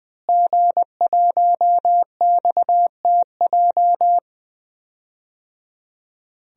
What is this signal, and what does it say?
Morse code, heard as a keyed tone, about 21 wpm, 700 Hz. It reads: G1XTJ
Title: DXCC-100 Call Signs - 5 Characters - Repeated 3x - 20wpm